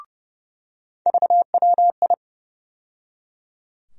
Morse code, heard as a keyed tone, about 30 wpm, 700 Hz. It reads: VWI
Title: USA Call Sign Postfixes - 3 Letters 30wpm